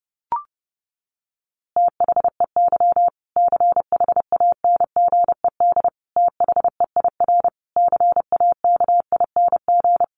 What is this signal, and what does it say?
Morse code, heard as a keyed tone, about 30 wpm, 700 Hz. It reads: THEY CHANGED THEIR CAKING
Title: A Sound Of Thunder 30wpm